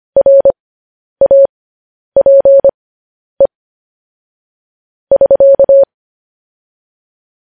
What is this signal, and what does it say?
Morse code, heard as a keyed tone, about 25 wpm, spaced as for 10 wpm, 550 Hz. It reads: RAPE <SK>
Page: news-headlines-in-morse-code-at-10-wpm